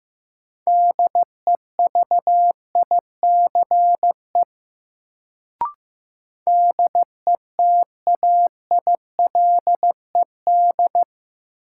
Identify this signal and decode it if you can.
Morse code, heard as a keyed tone, about 15 wpm, 700 Hz. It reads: DEVICE DETAILED
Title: Sentences from Top 1500 Words - New Word Drill - 15wpm